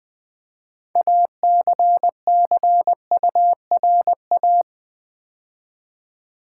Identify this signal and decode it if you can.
Morse code, heard as a keyed tone, about 20 wpm, 700 Hz. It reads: ACCURA
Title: Build A Word - 2 Letters At A Time - 8-Letter Words - 20wpm